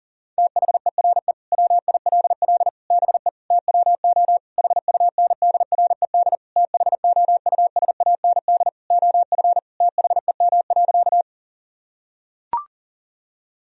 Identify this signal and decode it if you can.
Morse code, heard as a keyed tone, about 37 wpm, 700 Hz. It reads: THERE WILL BE TWO HUNDRED THOUSAND OF THEM.
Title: Sentences from Top 1000 Words - Review Part 3 - 40wpm